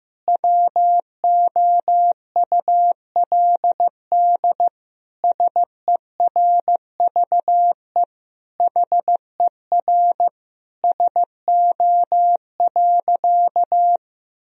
Morse code, heard as a keyed tone, about 15 wpm, 700 Hz. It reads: WOULD SERVE HER SO.
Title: Sentences from Top 1000 Words - Review Part 2 - 15wpm